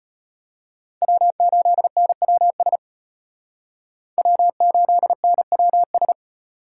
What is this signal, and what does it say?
Morse code, heard as a keyed tone, about 38 wpm, 700 Hz. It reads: W8NWS W8NWS